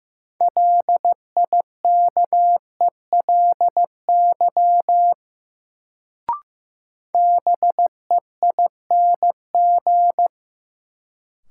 Morse code, heard as a keyed tone, about 15 wpm, 700 Hz. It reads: LIKELY BEING
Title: Sentences from Top 600 Words - New Word Drill - 15wpm